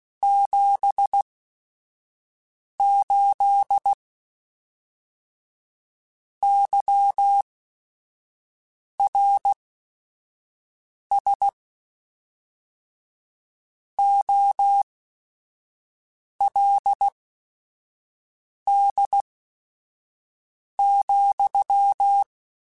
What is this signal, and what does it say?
Morse code, heard as a keyed tone, about 16 wpm, spaced as for 5 wpm, 800 Hz. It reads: 78 YRS OLD,
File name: QSOs / QSOs-mp3s / kc7sac@5